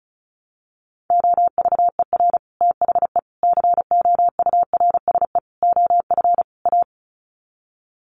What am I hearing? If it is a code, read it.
Morse code, heard as a keyed tone, about 34 wpm, 700 Hz. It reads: OVER THE COURSE OF A
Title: Sets of 5 Words Encore 35wpm